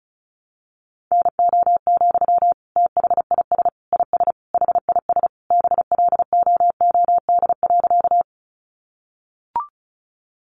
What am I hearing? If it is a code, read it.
Morse code, heard as a keyed tone, about 34 wpm, 700 Hz. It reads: NO, THIS IS HIS BLOOD.
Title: Sentences from Top 1100 Words 35wpm